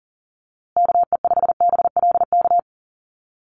Morse code, heard as a keyed tone, about 40 wpm, 700 Hz. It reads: KE5BLK
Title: USA Call Signs Repeated 2x 40wpm